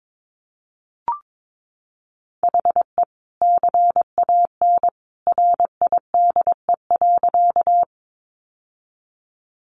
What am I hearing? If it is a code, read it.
Morse code, heard as a keyed tone, about 22 wpm, 700 Hz. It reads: HE CAN RIDE.
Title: Sentences from Top 1200 Words 22wpm